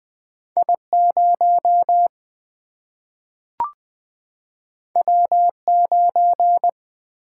Morse code, heard as a keyed tone, about 20 wpm, 700 Hz. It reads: I0 W9
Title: Single-Letter Single-Number 20wpm